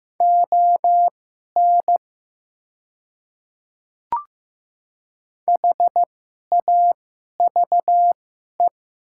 Morse code, heard as a keyed tone, about 15 wpm, spaced as for 11 wpm, 700 Hz. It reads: ON HAVE
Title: 2x Character Spacing - Top 100 Words - 15wpm